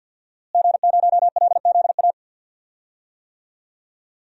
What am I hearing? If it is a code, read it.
Morse code, heard as a keyed tone, about 52 wpm, 700 Hz. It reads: G0LZA
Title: DXCC-100 Call Signs - 5 Characters - Repeated 2x - 50wpm